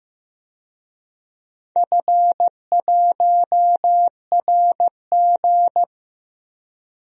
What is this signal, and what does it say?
Morse code, heard as a keyed tone, about 15 wpm, 700 Hz. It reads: F1RG
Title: DXCC-100 Call Signs - 4 Characters - 15wpm